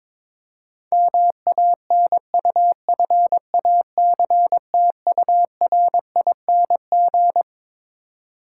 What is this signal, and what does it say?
Morse code, heard as a keyed tone, about 22 wpm, 700 Hz. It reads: MANUFACTURING
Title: Practice QSO - Exchange 3 - More Info - 22wpm